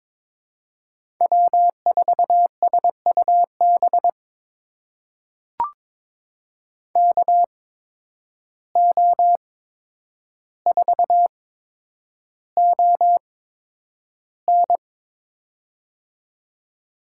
Morse code, heard as a keyed tone, about 22 wpm, spaced as for 9 wpm, 700 Hz. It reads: W4SUB KO4ON
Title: USA Call Signs - 5 Characters - 8x Character Spacing - 22wpm